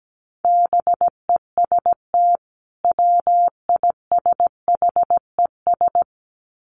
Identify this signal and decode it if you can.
Morse code, heard as a keyed tone, about 17 wpm, 700 Hz. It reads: BEST WISHES
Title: Common QSO Phrases 17wpm